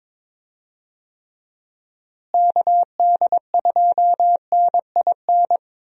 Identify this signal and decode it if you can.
Morse code, heard as a keyed tone, about 22 wpm, 700 Hz. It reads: KD2NIN
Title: USA Call Signs - 6 Characters - Repeated 3x - 22wpm